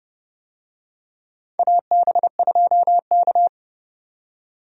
Morse code, heard as a keyed tone, about 29 wpm, 700 Hz. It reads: AB2K
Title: USA Call Signs - 4 Characters - 30wpm